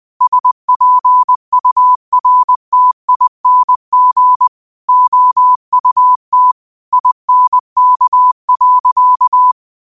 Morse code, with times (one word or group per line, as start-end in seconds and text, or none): 0.2-4.5 SPURTING
4.9-6.5 OUT
6.9-9.5 INK.